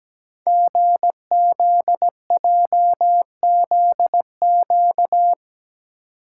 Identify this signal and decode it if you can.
Morse code, heard as a keyed tone, about 17 wpm, 700 Hz. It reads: GZJZQ
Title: Code Groups - 5 Letter:Numbers - 17wpm